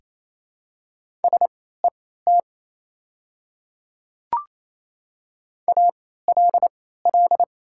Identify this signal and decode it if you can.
Morse code, heard as a keyed tone, about 27 wpm, spaced as for 16 wpm, 700 Hz. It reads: SET ALL
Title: 3x Character Spacing - Top 200 Words - 28wpm